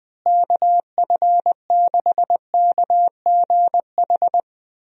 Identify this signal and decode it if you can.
Morse code, heard as a keyed tone, about 20 wpm, 700 Hz. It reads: KF6KGH
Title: USA Call Signs - 6 Characters - Repeated 3x - 20wpm